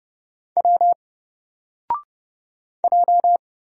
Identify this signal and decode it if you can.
Morse code, heard as a keyed tone, about 30 wpm, 700 Hz. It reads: W J
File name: Single-Letter-Number-30wpm